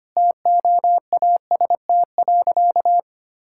Morse code, heard as a keyed tone, about 25 wpm, 700 Hz. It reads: TOAST.
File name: Easy Sentences with Hints - 25wpm